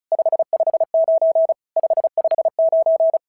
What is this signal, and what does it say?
Morse code, heard as a keyed tone, about 34 wpm, 650 Hz. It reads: 559 559
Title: SOTA QSO Practice 35wpm